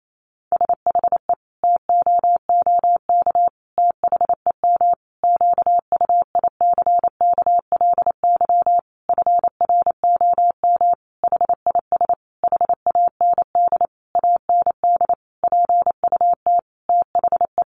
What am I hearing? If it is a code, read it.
Morse code, heard as a keyed tone, about 28 wpm, 700 Hz. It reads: SHE TOOK THEM QUICKLY FROM HIS HAND AND PUT THE
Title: Sentences from Top 1300 Words 28wpm